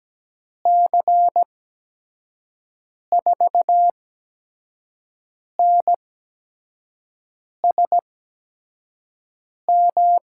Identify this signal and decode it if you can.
Morse code, heard as a keyed tone, about 17 wpm, spaced as for 5 wpm, 700 Hz. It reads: C4NSM
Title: USA Call Signs - 6 Characters - 8x Character Spacing - 17wpm